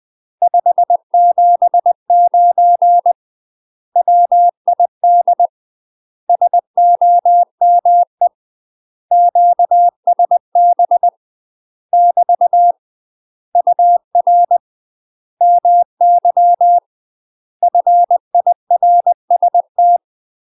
Morse code, heard as a keyed tone, about 20 wpm, 700 Hz. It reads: 579 WID SOME QSB <BT> UR MY FIRST
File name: UK MASTER QSOs TRAINING / 20WPM / UK MASTER QSOs 0068